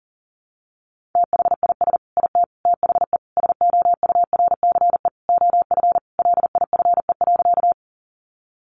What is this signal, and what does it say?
Morse code, heard as a keyed tone, about 40 wpm, 700 Hz. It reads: THIS IT THE SOURCE OF LIFE.
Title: Sentences from Top 800 Words 40wpm